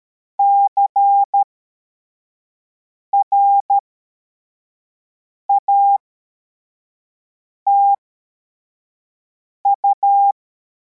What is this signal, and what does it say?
Morse code, heard as a keyed tone, about 13 wpm, spaced as for 4 wpm, 800 Hz. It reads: CRATU
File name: formation / cw / Text44